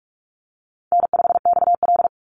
Morse code, heard as a keyed tone, about 44 wpm, 700 Hz. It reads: N5XL
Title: NA SOTA Chaser Callsigns 45wpm